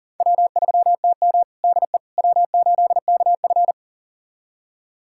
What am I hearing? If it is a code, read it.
Morse code, heard as a keyed tone, about 40 wpm, 700 Hz. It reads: W3TM DE W8KF
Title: USA Callsign DE Callsign - 4 Characters - 40wpm